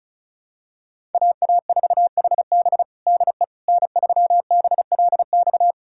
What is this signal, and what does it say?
Morse code, heard as a keyed tone, about 36 wpm, 700 Hz. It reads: AA4HB DE N3BLX
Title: USA Callsign DE Callsign - 5 Characters - 35wpm